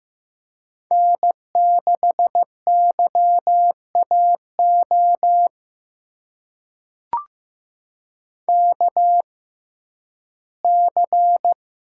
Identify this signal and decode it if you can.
Morse code, heard as a keyed tone, about 15 wpm, spaced as for 11 wpm, 700 Hz. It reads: N6YAO KC
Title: USA Call Signs - 5 Characters - 6x Character Spacing - 15wpm